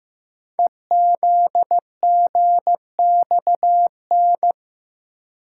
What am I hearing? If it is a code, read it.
Morse code, heard as a keyed tone, about 15 wpm, 700 Hz. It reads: EZGXN
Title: Code Groups - 5 Letter:Numbers - 15wpm